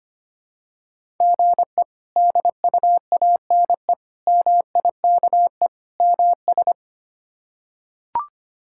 Morse code, heard as a keyed tone, about 25 wpm, 700 Hz. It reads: GE DUANE MIKE MS
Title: K1USN SST QSO Practice - 25wpm